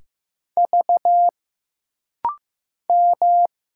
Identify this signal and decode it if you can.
Morse code, heard as a keyed tone, about 15 wpm, 700 Hz. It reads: V M
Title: Single Letters - Rapid-Fire 15wpm